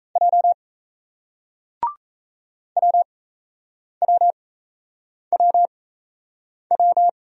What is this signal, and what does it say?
Morse code, heard as a keyed tone, about 38 wpm, spaced as for 8 wpm, 700 Hz. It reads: J WWWW